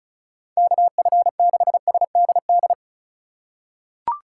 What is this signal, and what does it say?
Morse code, heard as a keyed tone, about 34 wpm, 700 Hz. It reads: KF6SDD
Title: USA Call Signs - 3x Character Spacing - 35wpm